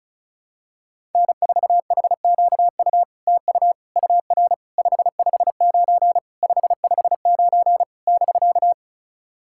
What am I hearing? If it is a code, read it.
Morse code, heard as a keyed tone, about 36 wpm, 700 Hz. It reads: N4HQU TU UR 559 559 <BK>
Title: SOTA QSO Practice 35wpm